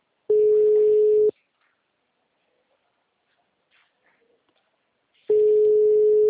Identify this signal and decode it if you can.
Ringback tone (Europe, 425 Hz).